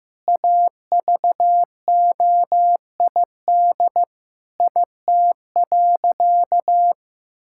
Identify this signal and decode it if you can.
Morse code, heard as a keyed tone, about 15 wpm, 700 Hz. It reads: AVOID IT.
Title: Sentences from Top 1000 Words - Review Part 6 - 15wpm